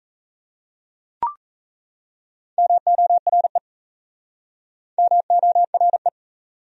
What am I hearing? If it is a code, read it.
Morse code, heard as a keyed tone, about 38 wpm, 700 Hz. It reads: MORE MORE